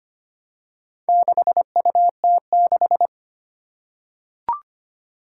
Morse code, heard as a keyed tone, about 25 wpm, 700 Hz. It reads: 6UT6
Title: Four Challenging Letter-Number Combinations 25wpm